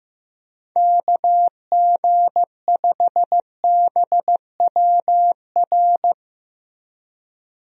Morse code, heard as a keyed tone, about 16 wpm, 700 Hz. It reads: KG5BWR
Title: USA Call Signs - 6 Characters - Repeated 3x - 15wpm